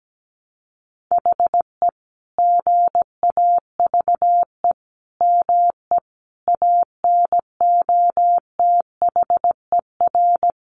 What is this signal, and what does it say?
Morse code, heard as a keyed tone, about 17 wpm, 700 Hz. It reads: HE GAVE ME ANOTHER
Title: Sentences from Top 1300 Words 17wpm